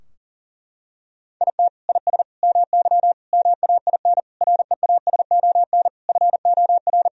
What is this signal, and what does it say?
Morse code, heard as a keyed tone, about 40 wpm, 700 Hz. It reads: IT IS MY MAIN REASON FOR
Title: Sentences from Top 1000 Words - Review Part 2 - 40wpm